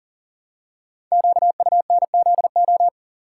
Morse code, heard as a keyed tone, about 39 wpm, 700 Hz. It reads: QUNZO
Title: Code Groups - 5 Letters - 40wpm